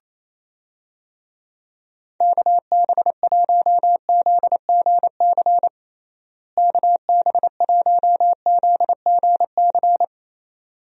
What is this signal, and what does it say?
Morse code, heard as a keyed tone, about 28 wpm, 700 Hz. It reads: KB1ZGC KB1ZGC
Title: USA Call Signs - 6 Characters - Repeated 2x - 28wpm